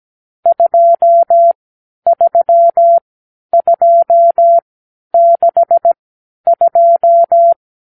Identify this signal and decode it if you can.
Morse code, heard as a keyed tone, about 17 wpm, spaced as for 10 wpm, 680 Hz. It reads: 23262
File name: k7qo / Lesson 50 (1-6)